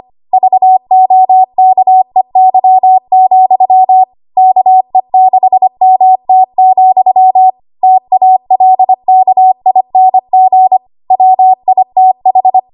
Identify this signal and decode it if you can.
Morse code, heard as a keyed tone, about 25 wpm, 750 Hz. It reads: VOKEY, KE6MT, TALKING WITH
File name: Morse / 25WPM